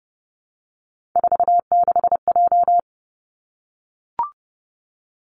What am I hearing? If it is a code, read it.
Morse code, heard as a keyed tone, about 30 wpm, 700 Hz. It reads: 46J
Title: Three Challenging Letter-Number Combinations 30wpm